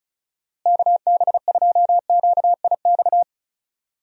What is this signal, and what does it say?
Morse code, heard as a keyed tone, about 36 wpm, 700 Hz. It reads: KB2QIX
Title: USA Call Signs - 6 Characters - Repeated 3x - 35wpm